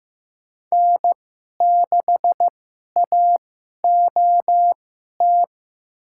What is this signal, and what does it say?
Morse code, heard as a keyed tone, about 15 wpm, spaced as for 11 wpm, 700 Hz. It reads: N6AOT
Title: USA Call Signs - 2x Character Spacing - 15wpm